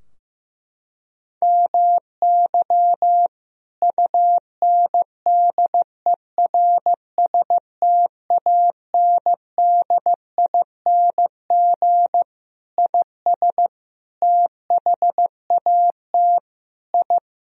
Morse code, heard as a keyed tone, about 15 wpm, 700 Hz. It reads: MY UNDERSTANDING IS THAT I
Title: Sentences from Top 1000 Words - Review Part 3 - 15wpm